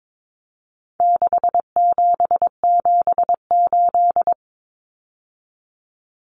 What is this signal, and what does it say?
Morse code, heard as a keyed tone, about 22 wpm, 700 Hz. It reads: 6778
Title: Numbers - 4-Digit 22wpm